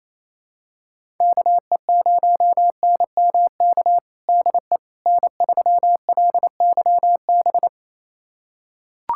Morse code, heard as a keyed tone, about 28 wpm, 700 Hz. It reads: KE0NMK DE N3LYB
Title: USA Callsign DE Callsign 28wpm